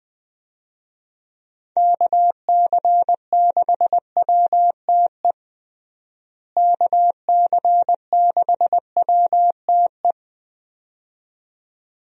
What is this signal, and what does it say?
Morse code, heard as a keyed tone, about 20 wpm, 700 Hz. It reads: KC6WTE KC6WTE
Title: USA Call Signs - 6 Characters - Repeated 2x - 20wpm